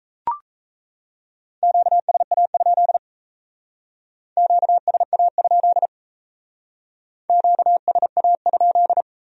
Morse code, heard as a keyed tone, about 35 wpm, 700 Hz. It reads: QSA? QSA? QSA?